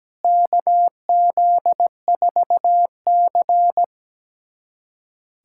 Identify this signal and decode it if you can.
Morse code, heard as a keyed tone, about 17 wpm, 700 Hz. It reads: KZ4C
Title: USA Callsign DE Callsign - 4 Characters - 17wpm